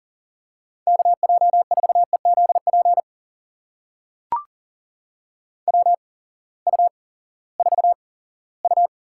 Morse code, heard as a keyed tone, about 40 wpm, spaced as for 20 wpm, 700 Hz. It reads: KJ4EZP WU4U
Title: USA Call Signs - 8x Character Spacing - 40wpm